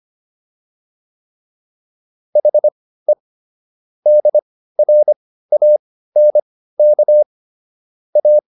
Morse code, heard as a keyed tone, about 25 wpm, spaced as for 15 wpm, 600 Hz. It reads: HE DRANK A